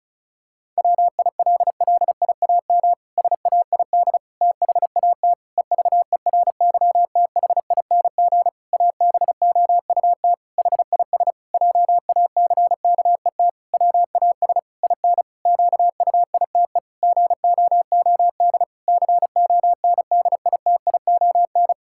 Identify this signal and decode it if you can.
Morse code, heard as a keyed tone, about 36 wpm, 700 Hz. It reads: WILLIAM SAID THAT EVERYTHING ABOUT HIS JACKET WAS IN QUITE GOOD CONDITION
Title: Pangram Challenge v2 - 35wpm